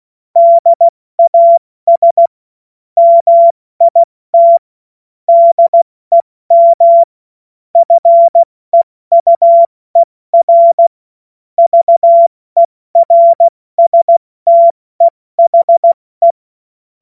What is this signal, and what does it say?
Morse code, heard as a keyed tone, about 16 wpm, spaced as for 14 wpm, 680 Hz. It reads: DAS MIT DEM FEUER VERSTEHE
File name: Morsekurs / Witze / Witz 02. Mai 2023 70 F80